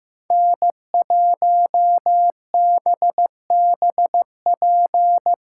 Morse code, heard as a keyed tone, about 17 wpm, 700 Hz. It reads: N1BBP
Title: Speed Racing - USA 1x3 Call Signs 15wpm